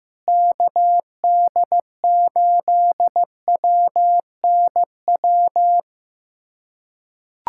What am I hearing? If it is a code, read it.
Morse code, heard as a keyed tone, about 15 wpm, 700 Hz. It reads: KD8WNW
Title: USA Callsign DE Callsign - 6 Characters - 15wpm